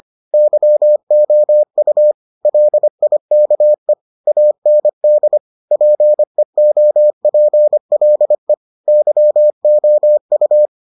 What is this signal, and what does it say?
Morse code, heard as a keyed tone, about 25 wpm, 600 Hz. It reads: YOU LIKE AND PEOPLE YOU